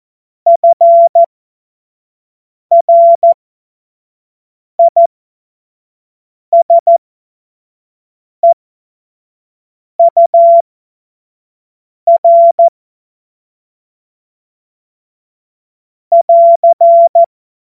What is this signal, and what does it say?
Morse code, heard as a keyed tone, about 14 wpm, spaced as for 5 wpm, 680 Hz. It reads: FRISEUR <AR>
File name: Morsekurs / Witze / Witz 06. Dezember 2022 25 F70